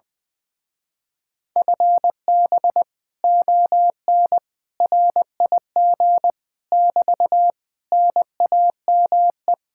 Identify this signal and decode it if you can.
Morse code, heard as a keyed tone, about 21 wpm, 700 Hz. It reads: FB ON RIG <BT> NAME
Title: Speed-Racing Common QSO Phrases - 20wpm